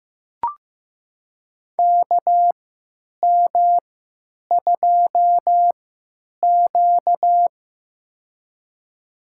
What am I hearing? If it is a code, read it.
Morse code, heard as a keyed tone, about 15 wpm, spaced as for 9 wpm, 700 Hz. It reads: KM2Q
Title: USA Call Signs - 4 Characters - 3x Character Spacing - 15wpm